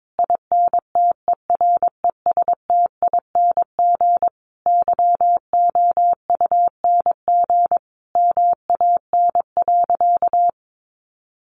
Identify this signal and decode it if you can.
Morse code, heard as a keyed tone, about 22 wpm, 700 Hz. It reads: INTERESTING YOUNG MAN.
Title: Sentences from Top 1000 Words 22wpm